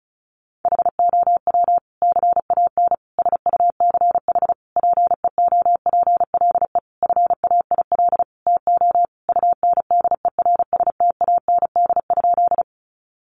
Morse code, heard as a keyed tone, about 34 wpm, 700 Hz. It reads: HOW CAN SUCH PEOPLE FAIL TO UNDERSTAND?
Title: Sentences from Top 1100 Words 35wpm